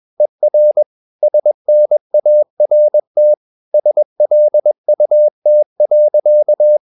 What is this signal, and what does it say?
Morse code, heard as a keyed tone, about 21 wpm, 600 Hz. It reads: ER SNART SLUT.
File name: Ferie QRS 21wpm